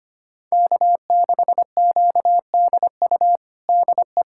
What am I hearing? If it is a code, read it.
Morse code, heard as a keyed tone, about 25 wpm, 700 Hz. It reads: K6QDU DE
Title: USA Callsign DE Callsign 25wpm